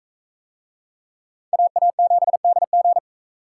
Morse code, heard as a keyed tone, about 44 wpm, 700 Hz. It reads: AA7DG